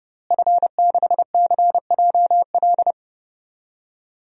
Morse code, heard as a keyed tone, about 32 wpm, 700 Hz. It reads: F6CJL
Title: DXCC-100 Call Signs - 5 Characters - Repeated 3x - 30wpm